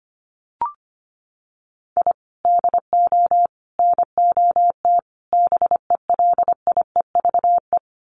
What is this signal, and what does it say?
Morse code, heard as a keyed tone, about 25 wpm, 700 Hz. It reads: I DO NOT BELIEVE
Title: Sets of 4 Words 25wpm